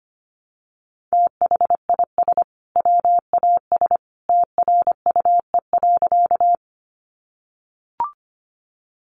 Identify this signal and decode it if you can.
Morse code, heard as a keyed tone, about 25 wpm, 700 Hz. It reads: THIS WAS TRUE.
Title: Sentences from Top 400 Words 25wpm